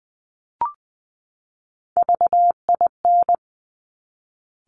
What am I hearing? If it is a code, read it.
Morse code, heard as a keyed tone, about 20 wpm, 700 Hz. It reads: VIN
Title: Top Three Letter Combinations in Words 20wpm